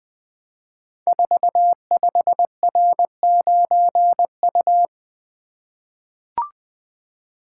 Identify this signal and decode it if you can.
Morse code, heard as a keyed tone, about 20 wpm, 700 Hz. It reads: 45R9U
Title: Code Groups - 5 Letter:Numbers - 20wpm